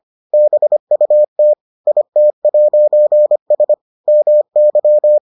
Morse code, heard as a keyed tone, about 25 wpm, 600 Hz. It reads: BUT IT'S MY